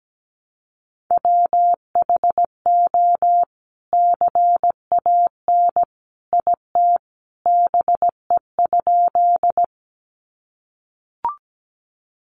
Morse code, heard as a keyed tone, about 17 wpm, 700 Hz. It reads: WHO CAN IT BE?
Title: Sentences from Top 100 Words 17wpm